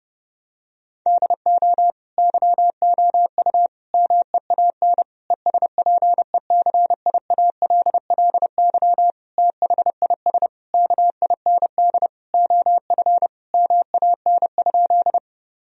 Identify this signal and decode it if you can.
Morse code, heard as a keyed tone, about 30 wpm, 700 Hz. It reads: DO YOU MEAN ESPECIALLY THIS KIND OF MAN?
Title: Sentences from Top 1000 Words - Review Part 4 - 30wpm